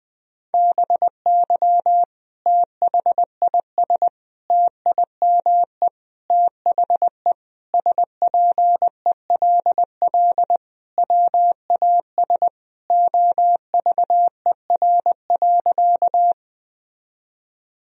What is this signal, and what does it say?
Morse code, heard as a keyed tone, about 21 wpm, 700 Hz. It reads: BY THIS TIME THE SPELL WAS OVER.
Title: Sentences from Top 1000 Words - Review Part 1 - 20wpm